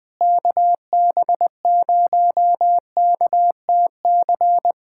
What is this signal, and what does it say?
Morse code, heard as a keyed tone, about 21 wpm, 700 Hz. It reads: KB0KTC
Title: USA Callsign DE Callsign 20wpm